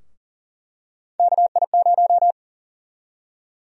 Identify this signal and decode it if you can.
Morse code, heard as a keyed tone, about 40 wpm, 700 Hz. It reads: KI0
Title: USA Call Sign Prefixes - Repeated 2x - 40wpm